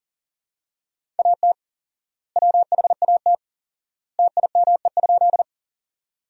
Morse code, heard as a keyed tone, about 40 wpm, 700 Hz. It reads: AT WHAT TIME?
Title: 4x Word-Spacing - Sentences from Top 100 Words - 40wpm